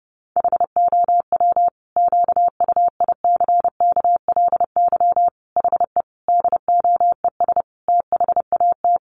Morse code, heard as a keyed tone, about 30 wpm, 700 Hz. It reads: HOW QUICKLY HE DOES THAT
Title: Sentences from Top 700 Words 30wpm